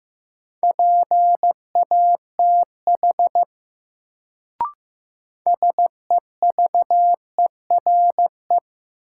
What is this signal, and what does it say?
Morse code, heard as a keyed tone, about 15 wpm, 700 Hz. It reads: PATH SEVERE
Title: Sentences from Top 1400 Words - New Word Drill Alt - 15wpm